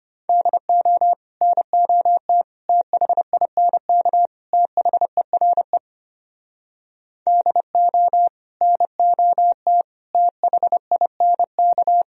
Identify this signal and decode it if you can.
Morse code, heard as a keyed tone, about 32 wpm, 690 Hz. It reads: DO NOT THINK THERE DO NOT THINK